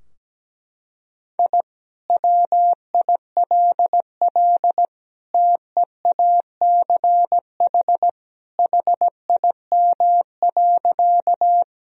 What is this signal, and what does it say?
Morse code, heard as a keyed tone, about 17 wpm, 700 Hz. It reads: I WILL TEACH HIM.
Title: Sentences from Top 1000 Words - Review Part 2 - 17wpm